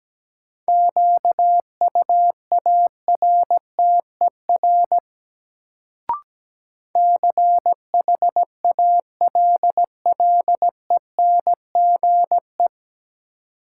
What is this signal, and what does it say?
Morse code, heard as a keyed tone, about 17 wpm, 700 Hz. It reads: QUARTER CHALLENGE
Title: Sentences from Top 1300 Words - New Word Drill Alt - 17wpm